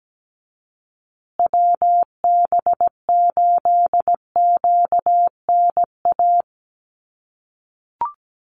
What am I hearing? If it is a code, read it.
Morse code, heard as a keyed tone, about 17 wpm, 700 Hz. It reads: WB8QNA
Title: USA Call Signs Repeated 3x 17wpm